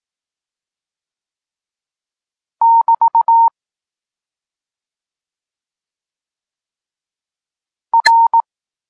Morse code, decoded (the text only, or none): <BT> R